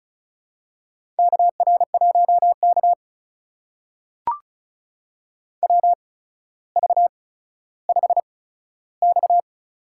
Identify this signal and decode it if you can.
Morse code, heard as a keyed tone, about 36 wpm, spaced as for 15 wpm, 700 Hz. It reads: KR1K WV5X
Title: USA Call Signs - 4 Characters - 8x Character Spacing - 35wpm